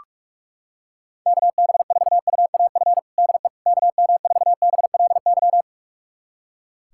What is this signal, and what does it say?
Morse code, heard as a keyed tone, about 46 wpm, 700 Hz. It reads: KB4UAF DE KM4BLY
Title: USA Callsign DE Callsign 45wpm